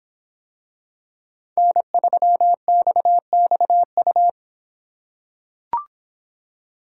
Morse code, heard as a keyed tone, about 24 wpm, 700 Hz. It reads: N3XXU